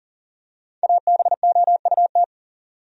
Morse code, heard as a keyed tone, about 40 wpm, 700 Hz. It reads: ABOUT
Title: Top 100 Words 40wpm